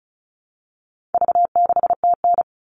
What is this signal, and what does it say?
Morse code, heard as a keyed tone, about 34 wpm, 700 Hz. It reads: V6TN
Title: All Cut Numbers - 4-Digit 35wpm